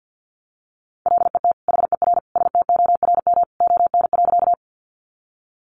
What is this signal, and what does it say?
Morse code, heard as a keyed tone, about 50 wpm, 700 Hz. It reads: LET HER STORM ON.
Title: Sentences from Top 1500 Words 50wpm